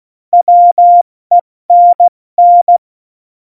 Morse code, heard as a keyed tone, about 16 wpm, spaced as for 14 wpm, 700 Hz. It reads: WENN
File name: Morsekurs / Witze / Witz 09. Januar 2018, 70 F80